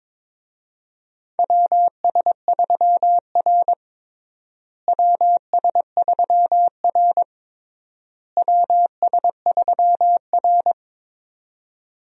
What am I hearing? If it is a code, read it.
Morse code, heard as a keyed tone, about 22 wpm, 700 Hz. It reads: WS3R WS3R WS3R
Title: USA Call Signs - 4 Characters - Repeated 3x - 22wpm